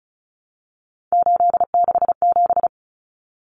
Morse code, heard as a keyed tone, about 34 wpm, 700 Hz. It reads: 867
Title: Common Cut Numbers - 3-Digit 35wpm